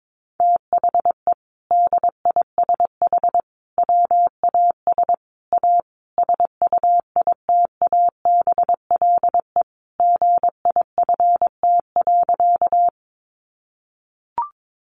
Morse code, heard as a keyed tone, about 22 wpm, 700 Hz. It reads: THE DISH WAS A SUITABLE GIFT.
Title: Sentences from Top 1400 Words 22wpm